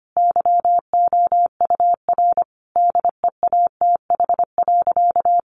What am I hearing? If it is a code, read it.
Morse code, heard as a keyed tone, about 25 wpm, 700 Hz. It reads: YOUR DEATH.
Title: Sentences from Top 1500 Words 25wpm